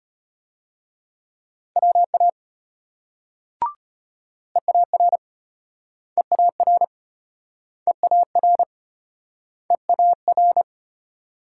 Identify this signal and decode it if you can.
Morse code, heard as a keyed tone, about 34 wpm, 700 Hz. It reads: WA EAR EAR EAR EAR